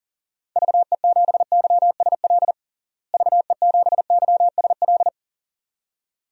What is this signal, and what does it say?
Morse code, heard as a keyed tone, about 40 wpm, 700 Hz. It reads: VE7YSL VE7YSL
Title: DXCC-100 Call Signs - 6 Characters - Repeated 2x - 40wpm